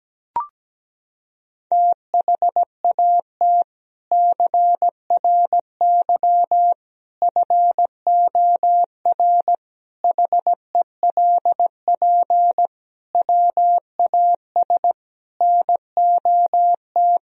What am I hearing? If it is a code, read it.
Morse code, heard as a keyed tone, about 17 wpm, 700 Hz. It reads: THAT CRY FOR HELP WAS NOT
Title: Sentences from Top 1000 Words - Review Part 3 - 17wpm